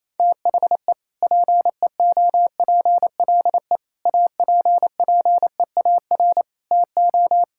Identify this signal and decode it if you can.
Morse code, heard as a keyed tone, about 28 wpm, 700 Hz. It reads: THE PEOPLE APPEAR TO
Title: Sentences from Top 1000 Words - Review Part 5 - 28wpm